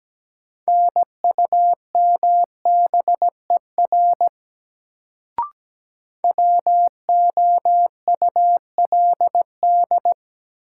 Morse code, heard as a keyed tone, about 17 wpm, 700 Hz. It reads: NUMBER WOULD
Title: Sentences from Top 100 Words - New Word Drill - 17wpm